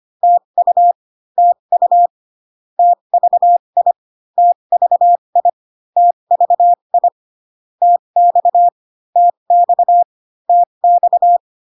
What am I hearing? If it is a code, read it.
Morse code, heard as a keyed tone, about 25 wpm, spaced as for 22 wpm, 700 Hz. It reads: TU TU TVI TVI TVI TX TX TX
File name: CW common abbreviations alphabetical repeated 3 times 25 18wpm